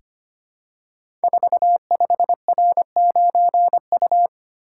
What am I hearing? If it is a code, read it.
Morse code, heard as a keyed tone, about 25 wpm, 700 Hz. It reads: 45R9U
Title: Code Groups - 5 Letter:Numbers - 25wpm